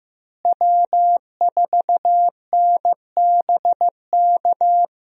Morse code, heard as a keyed tone, about 16 wpm, 700 Hz. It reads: W4NBK
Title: USA Call Signs - 5 Characters - Repeated 2x - 15wpm